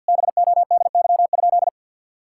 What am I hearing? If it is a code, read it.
Morse code, heard as a keyed tone, about 52 wpm, 700 Hz. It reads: BODY?
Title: Sentences from Top 1000 Words - Review Part 6 - 50wpm